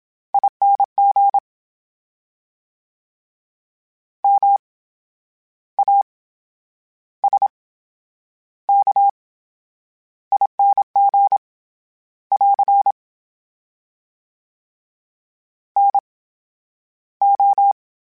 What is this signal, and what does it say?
Morse code, heard as a keyed tone, about 27 wpm, spaced as for 10 wpm, 800 Hz. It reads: ING MASKING <AR> NO
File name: Special INGs / Special INGs-mp3s / VerySlowToFaster Ing @25